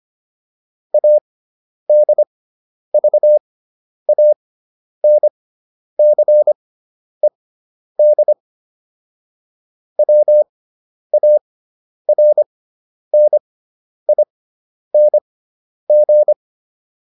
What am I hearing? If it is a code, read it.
Morse code, heard as a keyed tone, about 25 wpm, spaced as for 10 wpm, 600 Hz. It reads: ADVANCED WARNING